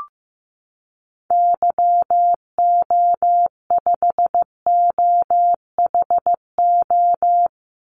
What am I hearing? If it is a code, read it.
Morse code, heard as a keyed tone, about 15 wpm, 700 Hz. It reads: YO5OHO
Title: Call Signs Repeated 3x - 15wpm